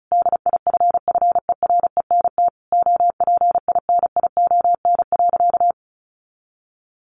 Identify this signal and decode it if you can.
Morse code, heard as a keyed tone, about 34 wpm, 700 Hz. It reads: DIFFERENT OPINION.
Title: Sentences from Top 1200 Words 35wpm